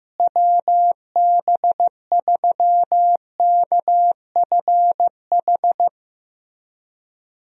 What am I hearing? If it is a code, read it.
Morse code, heard as a keyed tone, about 15 wpm, 700 Hz. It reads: WB3KFH
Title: USA Call Signs - 6 Characters - Repeated 2x - 15wpm